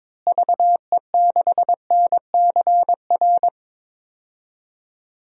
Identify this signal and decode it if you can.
Morse code, heard as a keyed tone, about 22 wpm, 700 Hz. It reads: VE6NCR
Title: DXCC-100 Call Signs - 6 Characters - Repeated 3x - 22wpm